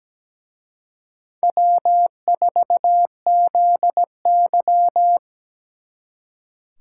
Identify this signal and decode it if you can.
Morse code, heard as a keyed tone, about 20 wpm, 700 Hz. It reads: W4ZY
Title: Speed Racing - USA Call Signs 17wpm